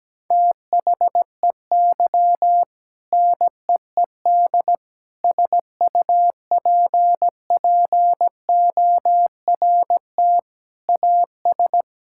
Morse code, heard as a keyed tone, about 17 wpm, 700 Hz. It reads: THEY NEED SUPPORT AS
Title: Sentences from Top 1000 Words - Review Part 5 - 17wpm